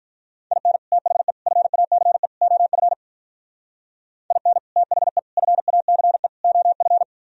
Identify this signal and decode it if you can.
Morse code, heard as a keyed tone, about 44 wpm, 700 Hz. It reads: IN THE FACE OF IN THE FACE OF